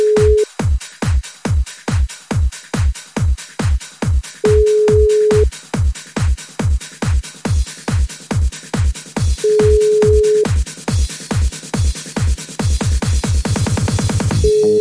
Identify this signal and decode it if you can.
Ringback tone (Europe, 425 Hz).